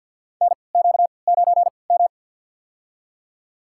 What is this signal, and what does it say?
Morse code, heard as a keyed tone, about 52 wpm, spaced as for 29 wpm, 690 Hz. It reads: NQ9M
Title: USA Call Signs - 3x Character Spacing - 50wpm